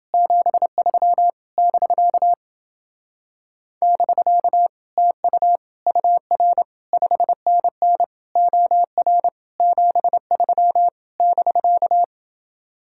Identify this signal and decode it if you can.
Morse code, heard as a keyed tone, about 30 wpm, 700 Hz. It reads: 73 <BK> <BK> TU UR 5NN OR 73 <BK>